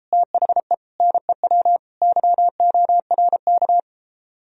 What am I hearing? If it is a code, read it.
Morse code, heard as a keyed tone, about 34 wpm, 700 Hz. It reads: THE NEW YORK